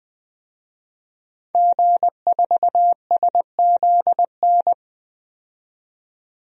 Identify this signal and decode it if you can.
Morse code, heard as a keyed tone, about 20 wpm, 700 Hz. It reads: G4SZN
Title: DXCC-100 Call Signs - 5 Characters - 20wpm